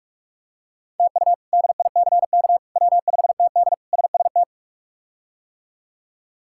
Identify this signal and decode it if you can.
Morse code, heard as a keyed tone, about 46 wpm, 700 Hz. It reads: TU DICK W5TD SST
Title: K1USN SST QSO Practice - 45wpm